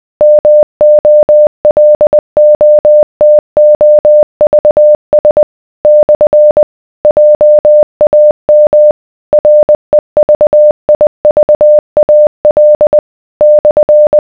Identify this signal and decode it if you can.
Morse code, heard as a keyed tone, about 20 wpm, 600 Hz. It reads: MOLOTOVS / JAM REVIVAL /